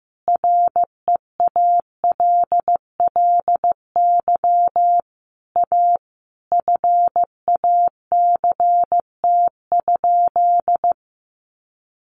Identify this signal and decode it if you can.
Morse code, heard as a keyed tone, about 15 wpm, 700 Hz. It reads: REALLY A FACT?
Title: Sentences from Top 600 Words 15wpm